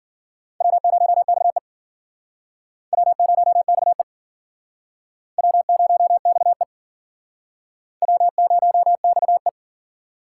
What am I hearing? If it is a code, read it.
Morse code, heard as a keyed tone, about 50 wpm, 700 Hz. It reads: W0XE W0XE W0XE W0XE